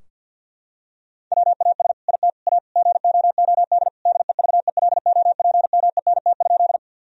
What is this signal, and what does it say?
Morse code, heard as a keyed tone, about 50 wpm, 700 Hz. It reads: WAS IT A GOOD DEVELOPMENT?
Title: Sentences from Top 1000 Words - Review Part 4 - 50wpm